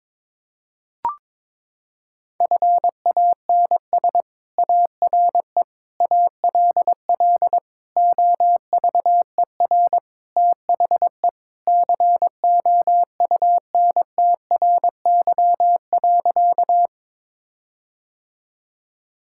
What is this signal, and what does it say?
Morse code, heard as a keyed tone, about 22 wpm, 700 Hz. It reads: FANS ARE ALL OVER THE COUNTRY.
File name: practice / 22wpm / Sentences with 4-Word Phrases - 22wpm